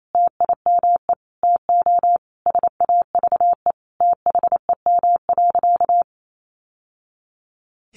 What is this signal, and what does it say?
Morse code, heard as a keyed tone, about 28 wpm, 700 Hz. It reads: TIME TO SAVE THEM.
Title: Sentences from Top 1100 Words 28wpm